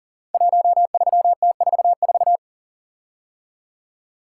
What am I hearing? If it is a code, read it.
Morse code, heard as a keyed tone, about 38 wpm, 700 Hz. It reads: 13T44
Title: Code Groups - 5 Letter:Numbers - 40wpm